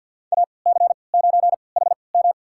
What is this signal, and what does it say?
Morse code, heard as a keyed tone, about 53 wpm, spaced as for 29 wpm, 700 Hz. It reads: AC9HM
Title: USA Call Signs - 5 Characters - 3x Character Spacing - 50wpm